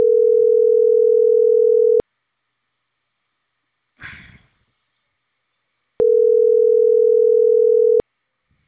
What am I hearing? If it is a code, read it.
Ringback tone (North America), 440 + 480 Hz.